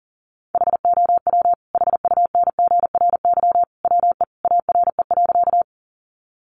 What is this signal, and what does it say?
Morse code, heard as a keyed tone, about 40 wpm, 700 Hz. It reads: HOW HUNGRY WE ARE.
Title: Sentences from Top 1400 Words 40wpm